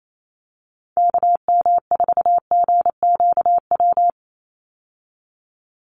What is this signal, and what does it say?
Morse code, heard as a keyed tone, about 28 wpm, 700 Hz. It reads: KM4GQW
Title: USA Call Signs 28wpm